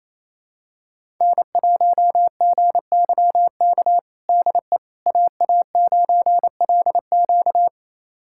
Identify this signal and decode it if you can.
Morse code, heard as a keyed tone, about 27 wpm, 700 Hz. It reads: N1GYK DE AA9LQ
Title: USA Callsign DE Callsign - 5 Characters - 28wpm